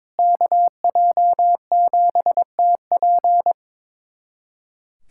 Morse code, heard as a keyed tone, about 22 wpm, 700 Hz. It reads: KJ7TP
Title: USA Callsign DE Callsign - 5 Characters - 22wpm